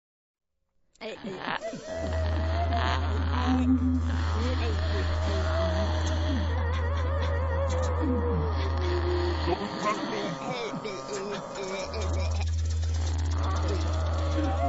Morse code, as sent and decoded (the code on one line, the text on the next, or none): none